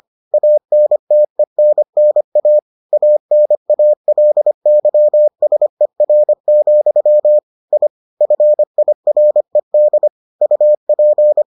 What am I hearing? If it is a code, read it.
Morse code, heard as a keyed tone, about 25 wpm, 600 Hz. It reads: ANTENNA ANALYSER, I FIRED UP